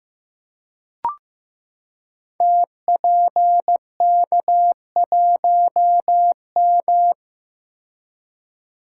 Morse code, heard as a keyed tone, about 15 wpm, 700 Hz. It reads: TPK1M
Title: Code Groups - 5 Letter:Numbers - 15wpm